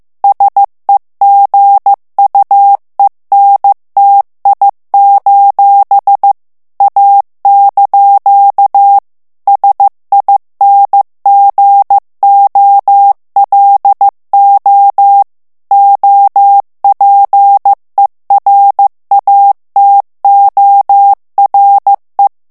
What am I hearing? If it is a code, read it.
Morse code, heard as a keyed tone, about 15 wpm, 800 Hz. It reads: SEGUENTI: A) SINGOLO OPERATORE
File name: morse mp3 / ITALIEN texte clair 2 vitesse 14 mots minute